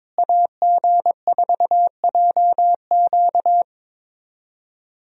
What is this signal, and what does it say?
Morse code, heard as a keyed tone, about 22 wpm, 700 Hz. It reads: AG4JQ
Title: USA Call Signs - 5 Characters - Repeated 2x - 22wpm